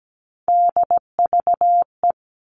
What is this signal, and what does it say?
Morse code, heard as a keyed tone, about 17 wpm, 700 Hz. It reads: DVE
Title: All Cut Numbers - 3-Digit 17wpm